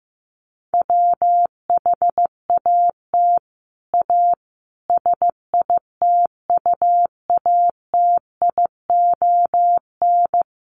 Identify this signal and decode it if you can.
Morse code, heard as a keyed tone, about 15 wpm, 700 Hz. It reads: WHAT A SITUATION
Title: Sentences from Top 1200 Words 15wpm